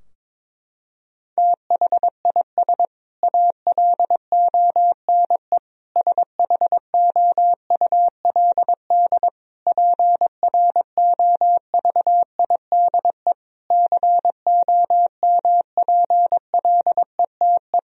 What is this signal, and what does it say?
Morse code, heard as a keyed tone, about 22 wpm, 700 Hz. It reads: THIS ALONE SHOULD PROVIDE COMPLETE
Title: Sentences from Top 1000 Words - Review Part 4 - 22wpm